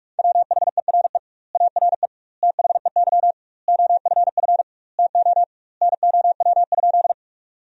Morse code, heard as a keyed tone, about 42 wpm, 700 Hz. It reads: WHERE ARE THEY OFF TO NOW?
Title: 2x Word-Spacing - Sentences from Top 200 Words - 45wpm